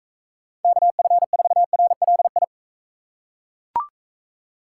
Morse code, heard as a keyed tone, about 39 wpm, 700 Hz. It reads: KF4RLI